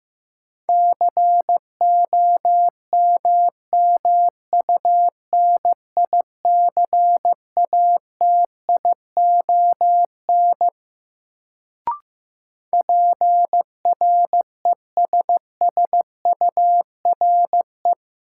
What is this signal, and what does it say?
Morse code, heard as a keyed tone, about 15 wpm, 700 Hz. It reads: COMMUNICATION PRESSURE
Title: Sentences from Top 1400 Words - New Word Drill Alt - 15wpm